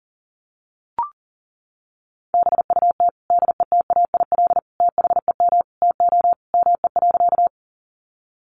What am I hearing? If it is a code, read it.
Morse code, heard as a keyed tone, about 40 wpm, 700 Hz. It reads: BUT DETAIL THEM TO ME.
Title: Sentences from Top 1500 Words 40wpm